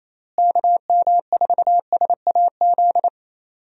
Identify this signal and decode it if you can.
Morse code, heard as a keyed tone, about 27 wpm, 700 Hz. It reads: KM4SAZ
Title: USA Call Signs - 6 Characters - Repeated 2x - 28wpm